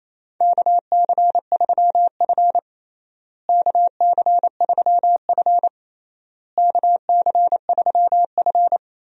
Morse code, heard as a keyed tone, about 28 wpm, 700 Hz. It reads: KC3F KC3F KC3F
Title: USA Call Signs - 4 Characters - Repeated 3x - 28wpm